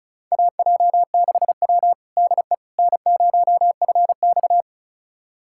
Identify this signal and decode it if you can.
Morse code, heard as a keyed tone, about 36 wpm, 700 Hz. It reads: AJ6W DE N0FX
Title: USA Callsign DE Callsign - 4 Characters - 35wpm